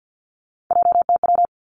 Morse code, heard as a keyed tone, about 50 wpm, 700 Hz. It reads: JTW
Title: Three Challenging Letter-Number Combinations 50wpm